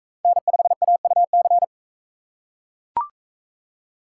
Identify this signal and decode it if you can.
Morse code, heard as a keyed tone, about 40 wpm, 700 Hz. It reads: N5AUC